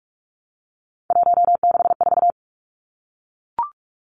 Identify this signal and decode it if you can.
Morse code, heard as a keyed tone, about 44 wpm, 700 Hz. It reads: 164
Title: Numbers - 3-Digit 45wpm